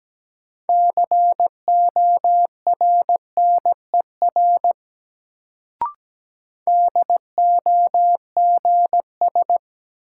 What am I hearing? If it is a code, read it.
Morse code, heard as a keyed tone, about 17 wpm, 700 Hz. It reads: CORNER DOGS
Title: Sentences from Top 1600 Words - New Word Drill - 17wpm